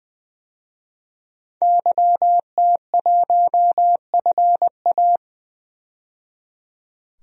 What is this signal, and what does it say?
Morse code, heard as a keyed tone, about 20 wpm, 700 Hz. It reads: YT1FA
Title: DXCC-100 Call Signs - 5 Characters - 20wpm